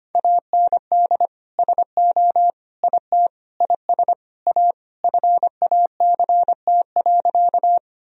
Morse code, heard as a keyed tone, about 25 wpm, 700 Hz. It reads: AND SO IT IS A FACT.
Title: Sentences from Top 1000 Words - Review Part 3 - 25wpm